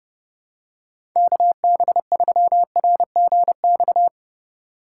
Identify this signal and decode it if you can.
Morse code, heard as a keyed tone, about 30 wpm, 700 Hz. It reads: KB3RGX
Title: USA Call Signs - 3x Character Spacing - 30wpm